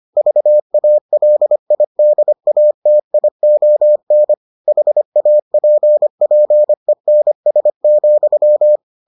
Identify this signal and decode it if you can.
Morse code, heard as a keyed tone, about 25 wpm, 600 Hz. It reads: VALIDATION HAPPENS,